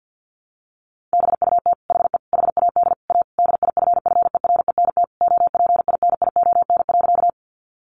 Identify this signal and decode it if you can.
Morse code, heard as a keyed tone, about 50 wpm, 700 Hz. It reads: BUT HE HAD A DIFFERENT OPINION.
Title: Sentences from Top 1200 Words 50wpm